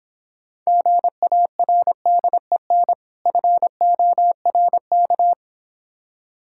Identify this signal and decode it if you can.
Morse code, heard as a keyed tone, about 26 wpm, 700 Hz. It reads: GARDEN FORK